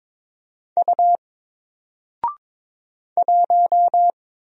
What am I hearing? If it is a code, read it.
Morse code, heard as a keyed tone, about 22 wpm, 700 Hz. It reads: U 1
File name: Single-Letter-Number-22wpm